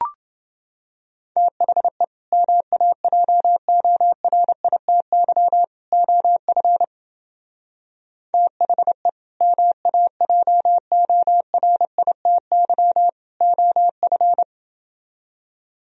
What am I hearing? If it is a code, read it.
Morse code, heard as a keyed tone, about 29 wpm, 700 Hz. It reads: THE MAJORITY OF THE MAJORITY OF